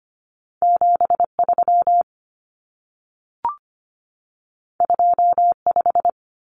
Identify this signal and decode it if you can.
Morse code, heard as a keyed tone, about 25 wpm, 700 Hz. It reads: 73 25
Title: Numbers - 2-Digit 25wpm